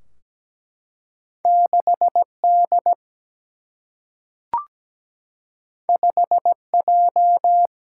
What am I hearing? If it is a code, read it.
Morse code, heard as a keyed tone, about 17 wpm, 700 Hz. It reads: 6 D 5 J
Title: Single-Letter Single-Number 17wpm